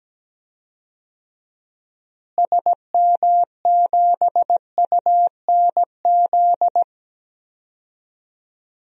Morse code, heard as a keyed tone, about 17 wpm, 700 Hz. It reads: SM7UNZ
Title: DXCC-100 Call Signs - 6 Characters - Repeated 2x - 17wpm